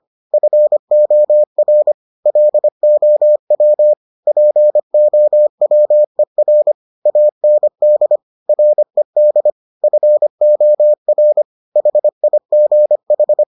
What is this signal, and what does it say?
Morse code, heard as a keyed tone, about 25 wpm, 600 Hz. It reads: FOR LOW POWER AND RED FOR HIGH